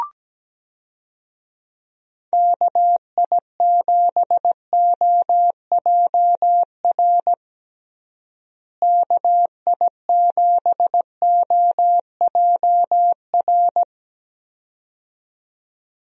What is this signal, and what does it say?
Morse code, heard as a keyed tone, about 17 wpm, 700 Hz. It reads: KI7OJR KI7OJR
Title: USA Call Signs - 6 Characters - Repeated 2x - 17wpm